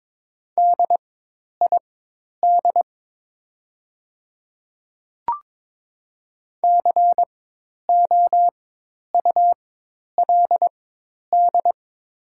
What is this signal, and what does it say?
Morse code, heard as a keyed tone, about 23 wpm, spaced as for 10 wpm, 700 Hz. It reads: DID COULD
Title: 4x Character Spacing - Top 100 Words - 22wpm